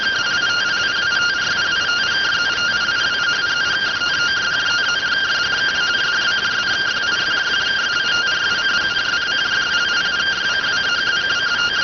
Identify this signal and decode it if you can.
Two-tone teleprinter signal, about 170 Hz shift.